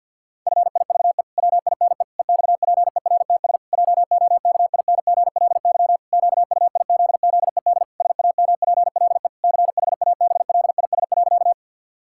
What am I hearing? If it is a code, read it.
Morse code, heard as a keyed tone, about 52 wpm, 700 Hz. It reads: FIVE WINE EXPERTS JOKINGLY QUIZZED SAMPLE CHABLIS.
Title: Pangram Challenge v2 - 50wpm